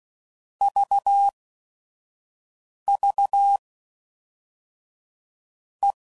Morse code, heard as a keyed tone, about 16 wpm, 2550 Hz. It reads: V V E
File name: Basic Letter Groups By7Groups / EISH5 / EISH5@5 f15